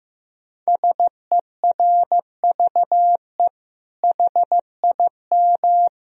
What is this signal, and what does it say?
Morse code, heard as a keyed tone, about 15 wpm, 700 Hz. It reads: SERVE HIM
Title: Sentences from Top 1000 Words - Review Part 2 - 15wpm